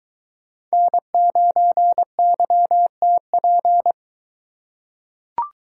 Morse code, heard as a keyed tone, about 21 wpm, 700 Hz. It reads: N9YTP